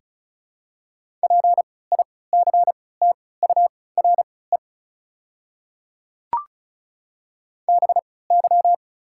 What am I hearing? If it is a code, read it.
Morse code, heard as a keyed tone, about 36 wpm, spaced as for 20 wpm, 700 Hz. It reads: PICTURE BY
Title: 3x Character Spacing - Top 200 Words - 35wpm